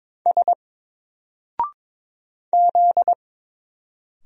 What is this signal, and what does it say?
Morse code, heard as a keyed tone, about 22 wpm, 700 Hz. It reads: S Z
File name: Single-Letter-Number-22wpm